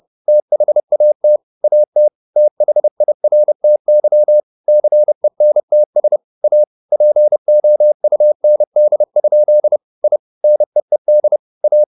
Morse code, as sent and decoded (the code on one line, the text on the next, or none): - .... .- - / .- - / - .... .. .-. - -.-- / -.-. -. - ... / .- / .--. --- ..- -. -.. ..--.. / .. / -. . -.. / .-
THAT AT THIRTY CENTS A POUND? I NEED A